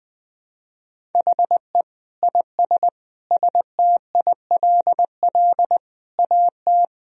Morse code, heard as a keyed tone, about 20 wpm, 700 Hz. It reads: HE IS STILL AT
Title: Sentences from Top 1000 Words - Review Part 1 - 20wpm